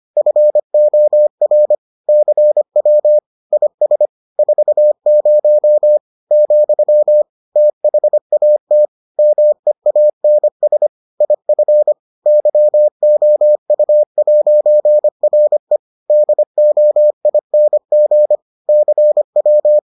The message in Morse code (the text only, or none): FOR CW IS 40 , THAT MEANS IF YOU'RE DOING CW